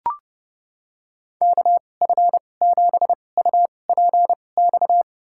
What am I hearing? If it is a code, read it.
Morse code, heard as a keyed tone, about 30 wpm, spaced as for 22 wpm, 700 Hz. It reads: KF7UPX
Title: USA Call Signs - 6 Characters - 2x Character Spacing - 30wpm